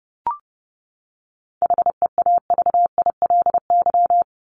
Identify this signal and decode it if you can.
Morse code, heard as a keyed tone, about 30 wpm, 700 Hz. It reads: HEAVILY
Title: LY Suffix - Top 100 Words with Base Words 30wpm